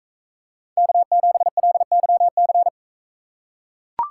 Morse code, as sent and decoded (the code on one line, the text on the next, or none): -.- --... .-.. -.-- -.-.
K7LYC